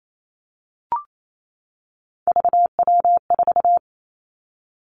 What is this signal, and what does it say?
Morse code, heard as a keyed tone, about 28 wpm, 700 Hz. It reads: VW4
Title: Three Challenging Letter-Number Combinations 28wpm